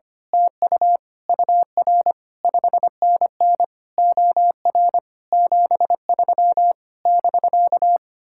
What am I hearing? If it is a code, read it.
Morse code, heard as a keyed tone, about 24 wpm, 700 Hz. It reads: TU UR 5NN OR 73 <BK>
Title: Speed-Racing Common QSO Phrases - 25wpm